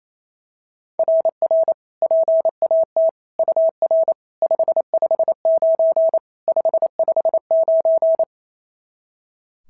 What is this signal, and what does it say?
Morse code, heard as a keyed tone, about 27 wpm, 660 Hz. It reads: RR PAT UR 559 559
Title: SOTA QSO Practice 28wpm